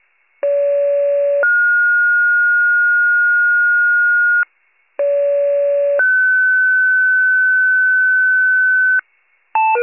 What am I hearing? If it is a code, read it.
Two-tone page, heard as a one-second tone, then a three-second tone, about 570 then 1430 Hz, about 570 then 1500 Hz. It is heard from 0.4 to 9.0 s.